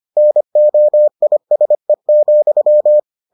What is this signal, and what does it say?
Morse code, heard as a keyed tone, about 25 wpm, 600 Hz. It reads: NOISE,